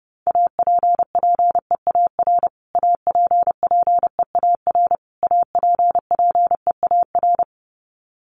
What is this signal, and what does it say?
Morse code, heard as a keyed tone, about 30 wpm, 700 Hz. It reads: APPEAR APPEAR APPEAR
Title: Top 500 Words Repeated 30wpm